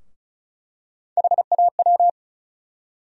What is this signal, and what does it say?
Morse code, heard as a keyed tone, about 34 wpm, 700 Hz. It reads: HAW
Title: USA Call Sign Postfixes - 3 Letters 35wpm